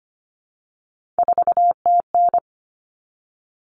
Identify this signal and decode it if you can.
Morse code, heard as a keyed tone, about 25 wpm, 700 Hz. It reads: 4TN
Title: All Cut Numbers - 3-Digit 25wpm